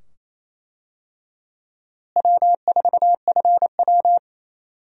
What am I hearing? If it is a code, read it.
Morse code, heard as a keyed tone, about 28 wpm, 700 Hz. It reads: W4FW
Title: USA Call Signs - 4 Characters - Repeated 2x - 28wpm